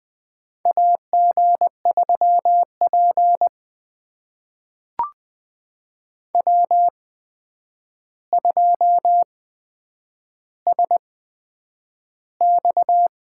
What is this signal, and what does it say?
Morse code, heard as a keyed tone, about 20 wpm, spaced as for 9 wpm, 700 Hz. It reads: AG3P W2SX
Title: USA Call Signs - 4 Characters - 8x Character Spacing - 20wpm